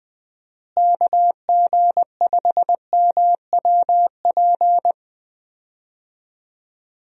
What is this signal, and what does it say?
Morse code, heard as a keyed tone, about 21 wpm, 700 Hz. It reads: KG5MWP
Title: USA Call Signs - 6 Characters - Repeated 3x - 20wpm